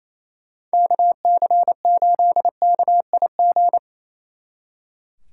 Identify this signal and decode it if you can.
Morse code, heard as a keyed tone, about 37 wpm, 700 Hz. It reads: KC8KIG